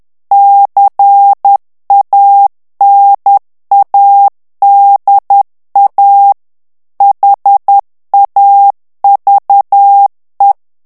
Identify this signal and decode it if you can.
Morse code, heard as a keyed tone, about 10 wpm, 800 Hz. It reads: CANADA HAVE
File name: morse mp3 / ANGLAIS texte clair 2 vitesse 10 mots minute